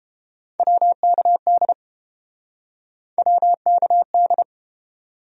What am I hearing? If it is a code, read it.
Morse code, heard as a keyed tone, about 30 wpm, 700 Hz. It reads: WKD WKD